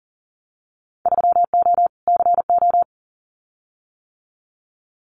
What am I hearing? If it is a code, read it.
Morse code, heard as a keyed tone, about 40 wpm, 700 Hz. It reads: 3O CO
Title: Winter Field Day Exchanges Encore 40wpm